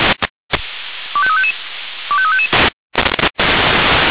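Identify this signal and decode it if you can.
Five-tone selective call (ZVEI), heard as quick runs of five tones, tones of about 70 ms. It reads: 25355 25355
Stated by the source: ZVEI five-tone selective call received on SDR.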